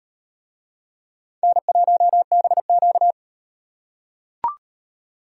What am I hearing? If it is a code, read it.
Morse code, heard as a keyed tone, about 38 wpm, 700 Hz. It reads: N1BQ